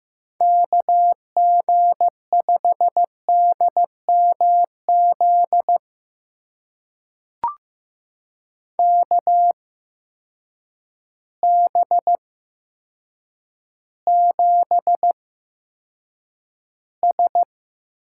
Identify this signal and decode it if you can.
Morse code, heard as a keyed tone, about 15 wpm, spaced as for 8 wpm, 700 Hz. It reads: KG5DMZ KB7S
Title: USA Call Signs - 6 Characters - 8x Character Spacing - 15wpm